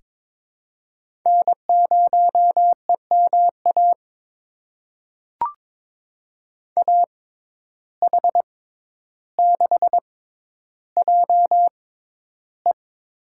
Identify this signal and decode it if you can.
Morse code, heard as a keyed tone, about 22 wpm, spaced as for 11 wpm, 700 Hz. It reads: N0EMA AH6JE
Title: USA Call Signs - 5 Characters - 6x Character Spacing - 22wpm